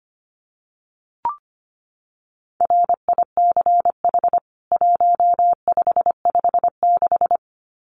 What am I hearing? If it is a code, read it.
Morse code, heard as a keyed tone, about 25 wpm, 700 Hz. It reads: RICH 1556
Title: CWT Exchange 25wpm